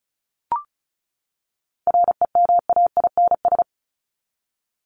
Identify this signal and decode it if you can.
Morse code, heard as a keyed tone, about 34 wpm, 700 Hz. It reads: REMAINS
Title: ES:S Suffix - Top 100 Words with Base Words 35wpm